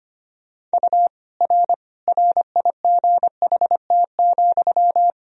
Morse code, heard as a keyed tone, about 25 wpm, 700 Hz. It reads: U R RIGHT,
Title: Sentences from Top 300 Words and Abbreviations 25wpm